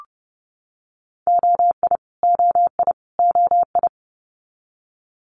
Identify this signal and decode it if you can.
Morse code, heard as a keyed tone, about 30 wpm, 700 Hz. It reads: OI OI OI
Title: Multi-letter Phonemes Repeated 3x 30wpm